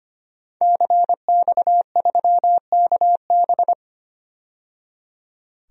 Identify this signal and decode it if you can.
Morse code, heard as a keyed tone, about 25 wpm, 700 Hz. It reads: CX3KB
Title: DXCC-100 Call Signs - 5 Characters - 25wpm